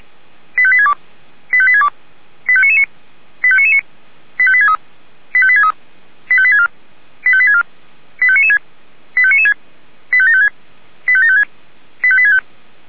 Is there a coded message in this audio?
Yes: five-tone selective call (ZVEI) from 0.6 to 12.4 s, heard as quick runs of five tones, tones of about 70 ms. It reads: 85862 85862 85999 85999 85863 85863 85864 85864 85996 85996 85756 85756 85865